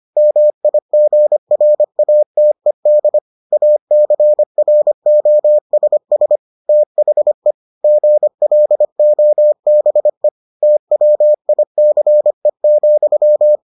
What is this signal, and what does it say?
Morse code, heard as a keyed tone, about 25 wpm, 600 Hz. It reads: MIGRATED ACROSS THE GLOBE TWICE,